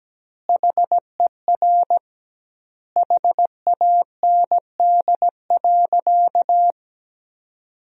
Morse code, heard as a keyed tone, about 17 wpm, 700 Hz. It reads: HER HAND.
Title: 2x Word-Spacing - Sentences from Top 500 Words - 17wpm